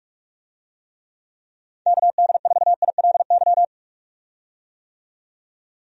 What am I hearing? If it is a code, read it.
Morse code, heard as a keyed tone, about 45 wpm, 700 Hz. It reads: KD4ILY
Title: USA Call Signs - 6 Characters - 45wpm